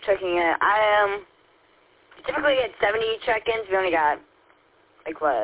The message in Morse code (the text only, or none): none